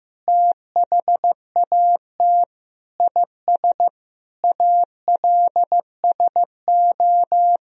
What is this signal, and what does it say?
Morse code, heard as a keyed tone, about 15 wpm, 700 Hz. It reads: THAT IS ALSO
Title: Sentences from Top 1000 Words - Review Part 1 - 15wpm